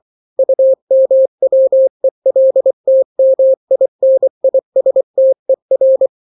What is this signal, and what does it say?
Morse code, heard as a keyed tone, about 24 wpm, 530 Hz. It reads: UMWELTMINISTER
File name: daily / 23wpm